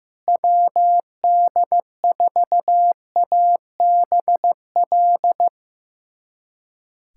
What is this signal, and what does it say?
Morse code, heard as a keyed tone, about 15 wpm, 700 Hz. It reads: WD4ABL
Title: USA Callsign DE Callsign 15wpm